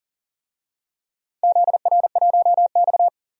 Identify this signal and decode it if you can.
Morse code, heard as a keyed tone, about 40 wpm, 700 Hz. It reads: ZR1X
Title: DXCC-100 Call Signs - 4 Characters - Repeated 3x - 40wpm